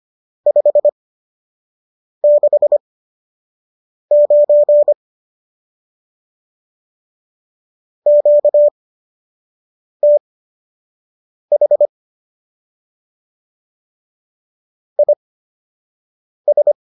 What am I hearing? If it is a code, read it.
Morse code, heard as a keyed tone, about 25 wpm, spaced as for 6 wpm, 590 Hz. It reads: 569 QTH IS